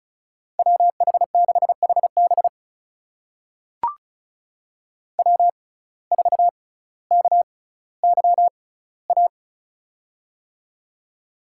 Morse code, heard as a keyed tone, about 36 wpm, spaced as for 18 wpm, 700 Hz. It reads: WH6HB W4KYA
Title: USA Call Signs - 5 Characters - 6x Character Spacing - 35wpm